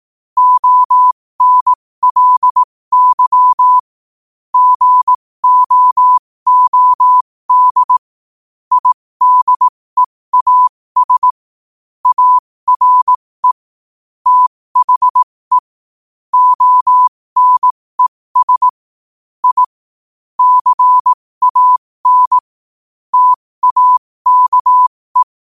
Morse code, heard as a keyed tone, about 18 wpm, spaced as for 15 wpm, 1000 Hz. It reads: ONLY GOOD IDEAS ARE THE ONES I CAN TAKE
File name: qotdfiles / 15wpm